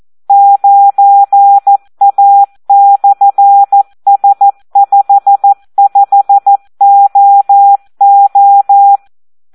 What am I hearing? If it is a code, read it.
Morse code, heard as a keyed tone, about 14 wpm, 800 Hz. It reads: 9A/S55OO
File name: Audio practica CW / SD 1 14